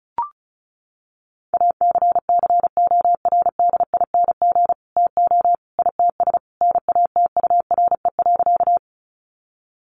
Morse code, heard as a keyed tone, about 34 wpm, 700 Hz. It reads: ACCORDING TO ITS NATURE.
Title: Sentences from Top 700 Words 35wpm